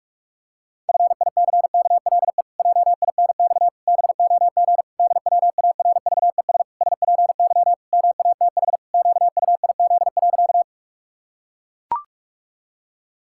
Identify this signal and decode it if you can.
Morse code, heard as a keyed tone, about 41 wpm, 700 Hz. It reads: FICKLE JINX BOG DWARVES SPY MATH QUIZ.
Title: Pangram Challenge v2 - 45wpm